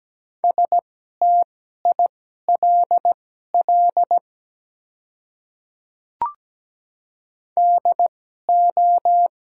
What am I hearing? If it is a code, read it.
Morse code, heard as a keyed tone, about 17 wpm, spaced as for 12 wpm, 690 Hz. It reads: STILL DO
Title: 2x Character Spacing - Top 300 Words - 17wpm